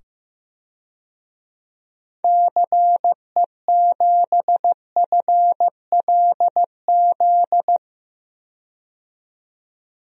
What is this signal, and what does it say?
Morse code, heard as a keyed tone, about 15 wpm, 700 Hz. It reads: CE7FLZ
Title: DXCC-100 Call Signs - 6 Characters - 15wpm